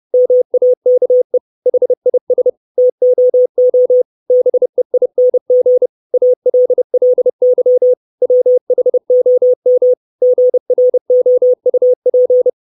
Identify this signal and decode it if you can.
Morse code, heard as a keyed tone, about 30 wpm, 500 Hz. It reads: MAKE HIS TOO BEING ALLY WHOM GROUP
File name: 632 Words 30WPM